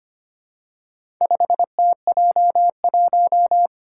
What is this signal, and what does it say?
Morse code, heard as a keyed tone, about 25 wpm, 700 Hz. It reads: 5TJ1
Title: Four Challenging Letter-Number Combinations 25wpm